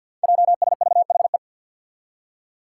Morse code, heard as a keyed tone, about 46 wpm, 700 Hz. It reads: JSVHE
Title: Five Challenging Letter-Number Combinations 50wpm